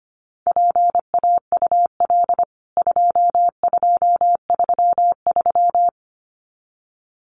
Morse code, heard as a keyed tone, about 25 wpm, 700 Hz. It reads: PAUL 2233
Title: CWT Exchange 25wpm